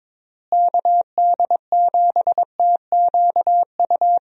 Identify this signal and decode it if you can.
Morse code, heard as a keyed tone, about 21 wpm, 700 Hz. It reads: KD7TQU
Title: USA Call Signs - 6 Characters - Repeated 3x - 22wpm